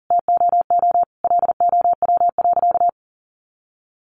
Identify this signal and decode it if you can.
Morse code, heard as a keyed tone, about 40 wpm, 700 Hz. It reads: TOO LOW.
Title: Sentences from Top 1200 Words 40wpm